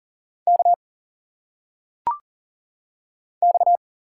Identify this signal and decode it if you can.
Morse code, heard as a keyed tone, about 40 wpm, 700 Hz. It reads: K X
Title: Single Letters - Phonetic Pronunciation 40wpm